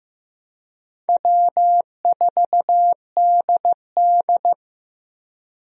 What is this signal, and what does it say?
Morse code, heard as a keyed tone, about 15 wpm, 700 Hz. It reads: W4DD
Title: USA Call Signs - 4 Characters - Repeated 3x - 15wpm